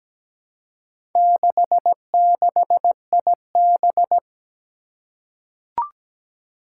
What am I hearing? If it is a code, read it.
Morse code, heard as a keyed tone, about 17 wpm, 700 Hz. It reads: 66IB
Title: Four Challenging Letter-Number Combinations 17wpm